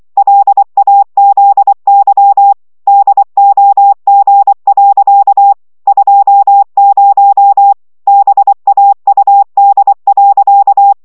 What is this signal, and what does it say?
Morse code, heard as a keyed tone, about 24 wpm, 800 Hz. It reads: LAZY DOG. 20 BAUD.
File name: szamolo / CW 20 baud